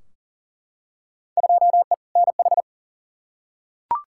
Morse code, heard as a keyed tone, about 40 wpm, 700 Hz. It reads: 2E NH
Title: Field Day Exchanges - 40wpm